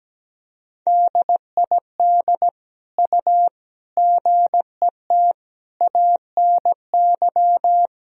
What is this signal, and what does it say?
Morse code, heard as a keyed tone, about 17 wpm, 700 Hz. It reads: DID U GET ANY
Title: Sentences from Top 400 Words and Abbreviations 17wpm